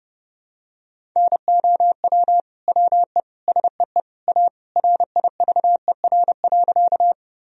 Morse code, heard as a keyed tone, about 32 wpm, 700 Hz. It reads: NOW WE SEE A RIVER.
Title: Sentences from Top 1000 Words - Review Part 1 - 30wpm